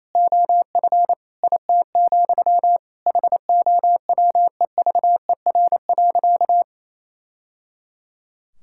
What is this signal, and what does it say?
Morse code, heard as a keyed tone, about 28 wpm, 700 Hz. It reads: OF IT, HOWEVER.
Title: Sentences from Top 1000 Words - Review Part 6 - 28wpm